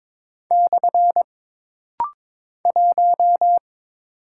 Morse code, heard as a keyed tone, about 22 wpm, 700 Hz. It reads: / 1
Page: This rapid-fire timing is geared to push your ICR (Instant Character Recognition) skills.